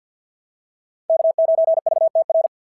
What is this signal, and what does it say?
Morse code, heard as a keyed tone, about 51 wpm, 690 Hz. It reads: K9VTR
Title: SOTA QSO Practice 50wpm